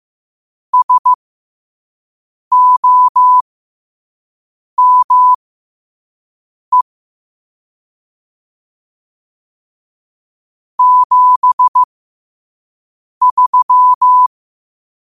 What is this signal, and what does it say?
Morse code, heard as a keyed tone, about 15 wpm, spaced as for 5 wpm, 1000 Hz. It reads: SOME 73
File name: qotdfiles / 5wpm